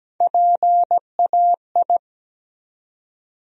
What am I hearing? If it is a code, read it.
Morse code, heard as a keyed tone, about 17 wpm, 700 Hz. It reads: PAI
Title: USA Call Sign Postfixes - 3 Letters 17wpm